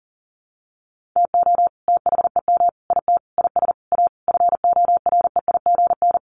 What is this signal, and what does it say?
Morse code, heard as a keyed tone, about 40 wpm, 700 Hz. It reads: TO THEM IT IS A FOREIGN
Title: Sentences from Top 1400 Words 40wpm